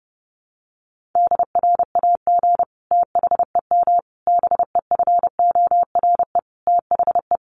Morse code, heard as a keyed tone, about 28 wpm, 700 Hz. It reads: DRAG THEM BEFORE THE
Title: Sentences from Top 1500 Words 30wpm